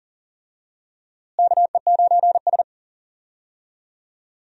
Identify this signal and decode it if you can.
Morse code, heard as a keyed tone, about 40 wpm, 700 Hz. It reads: KE9S
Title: USA Call Signs - 4 Characters - Repeated 2x - 40wpm